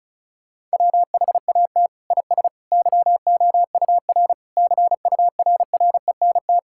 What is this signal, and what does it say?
Morse code, heard as a keyed tone, about 36 wpm, 700 Hz. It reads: WHAT IS YOUR CURRENT
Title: Sentences from Top 1000 Words - Review Part 6 - 35wpm